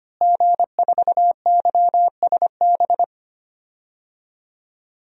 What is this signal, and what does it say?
Morse code, heard as a keyed tone, about 25 wpm, 700 Hz. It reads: G4YSB
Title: DXCC-100 Call Signs - 5 Characters - Repeated 2x - 25wpm